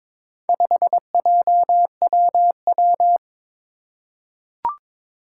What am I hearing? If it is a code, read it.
Morse code, heard as a keyed tone, about 22 wpm, 700 Hz. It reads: 5JWW
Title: Four Challenging Letter-Number Combinations 22wpm